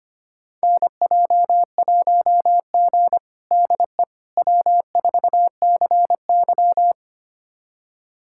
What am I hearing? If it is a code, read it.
Morse code, heard as a keyed tone, about 25 wpm, 700 Hz. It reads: NJ1G DE W4CY
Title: USA Callsign DE Callsign - 4 Characters - 25wpm